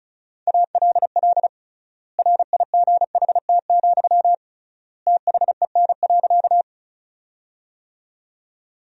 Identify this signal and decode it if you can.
Morse code, heard as a keyed tone, about 36 wpm, 700 Hz. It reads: ALL RIGHT, THEN.
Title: 3x Word-Spacing - Sentences from Top 200 Words - 35wpm